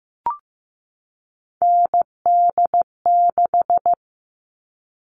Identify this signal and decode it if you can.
Morse code, heard as a keyed tone, about 15 wpm, 700 Hz. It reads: ND6
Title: All Cut Numbers - 3-Digit 15wpm